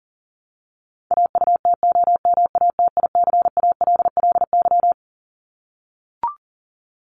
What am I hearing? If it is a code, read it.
Morse code, heard as a keyed tone, about 40 wpm, 700 Hz. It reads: AUTOMATICALLY
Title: Top 1751-2000 Words 40wpm